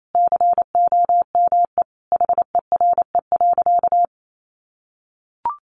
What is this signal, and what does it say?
Morse code, heard as a keyed tone, about 28 wpm, 700 Hz. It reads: COME HERE.
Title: Sentences from Top 1300 Words 28wpm